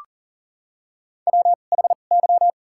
Morse code, heard as a keyed tone, about 40 wpm, spaced as for 29 wpm, 700 Hz. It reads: WHY
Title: 2x Character Spacing - Top 200 Words - 40wpm